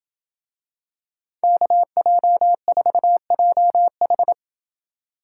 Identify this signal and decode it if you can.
Morse code, heard as a keyed tone, about 28 wpm, 700 Hz. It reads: KJ4JH